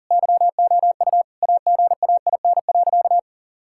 Morse code, heard as a keyed tone, about 40 wpm, 700 Hz. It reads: YOU AGAIN.
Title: Sentences from Top 1600 Words 40wpm